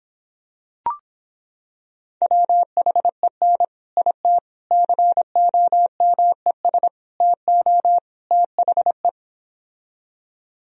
Morse code, heard as a keyed tone, about 26 wpm, 700 Hz. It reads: WHEN IT COMES TO THE